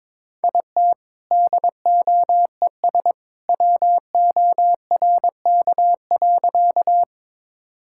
Morse code, heard as a keyed tone, about 22 wpm, 700 Hz. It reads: IT DOES WORK.
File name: practice / 22wpm / Sentences with 2-Word Phrases - 22wpm